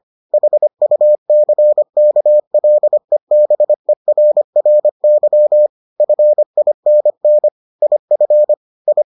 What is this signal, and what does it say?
Morse code, heard as a keyed tone, about 25 wpm, 600 Hz. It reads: HUCKLEBERRY FINN IF I